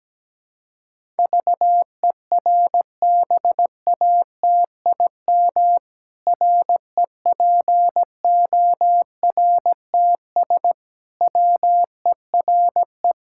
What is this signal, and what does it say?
Morse code, heard as a keyed tone, about 17 wpm, 700 Hz. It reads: VERBATIM REPORTS WERE
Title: Pangram Challenge v2 - 17wpm